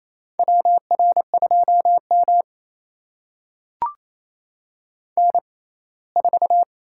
Morse code, heard as a keyed tone, about 28 wpm, spaced as for 19 wpm, 700 Hz. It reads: WR2M N4
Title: USA Call Signs - 6x Character Spacing - 28wpm